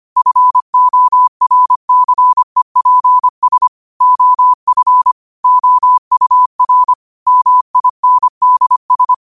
Morse code, heard as a keyed tone, about 25 wpm, 1000 Hz. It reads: FORCEPS OF OUR MINDS
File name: qotdfiles / 25wpm